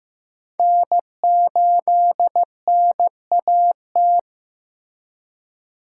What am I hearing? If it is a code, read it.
Morse code, heard as a keyed tone, about 15 wpm, 700 Hz. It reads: N8NAT
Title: USA Call Signs - 5 Characters - 3x Character Spacing - 15wpm